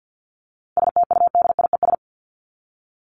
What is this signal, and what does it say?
Morse code, heard as a keyed tone, about 48 wpm, 700 Hz. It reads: STUDIES
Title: ES:S Suffix - Top 100 Words with Base Words 50wpm